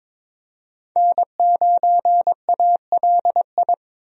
Morse code, heard as a keyed tone, about 21 wpm, 700 Hz. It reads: N9ALI
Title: USA Call Signs - 5 Characters - 22wpm